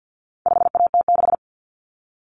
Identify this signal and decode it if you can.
Morse code, heard as a keyed tone, about 50 wpm, 700 Hz. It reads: 5AT6
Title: Common Cut Numbers - 4-Digit 50wpm